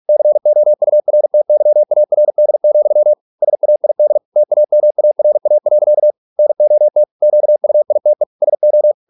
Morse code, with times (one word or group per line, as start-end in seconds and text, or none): 0.1-3.1 COURTYARD,
3.4-4.2 SAID
4.4-6.1 TAMARA.
6.4-7.0 NOT
7.2-8.2 QUITE
8.4-8.9 SO